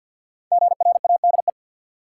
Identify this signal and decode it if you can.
Morse code, heard as a keyed tone, about 50 wpm, 700 Hz. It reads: GRADE
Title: Five Letter Words - 50wpm